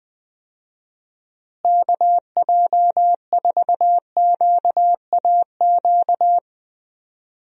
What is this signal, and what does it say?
Morse code, heard as a keyed tone, about 20 wpm, 700 Hz. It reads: KJ4QAQ
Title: USA Call Signs - 6 Characters - Repeated 2x - 20wpm